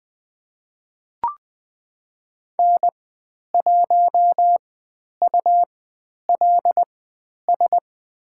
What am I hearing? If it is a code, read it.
Morse code, heard as a keyed tone, about 20 wpm, spaced as for 10 wpm, 700 Hz. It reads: N1ULS
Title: 102 Lesson 17g - Letter u - Call Signs 20wpm